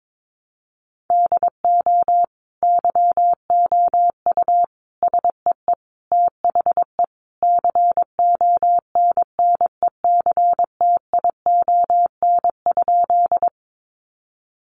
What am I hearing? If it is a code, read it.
Morse code, heard as a keyed tone, about 22 wpm, 700 Hz. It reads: DO YOU SEE THE CONNECTION?
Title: Sentences from Top 1300 Words 22wpm